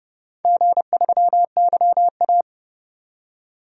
Morse code, heard as a keyed tone, about 32 wpm, 700 Hz. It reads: G3YA
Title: DXCC-100 Call Signs - 4 Characters - 30wpm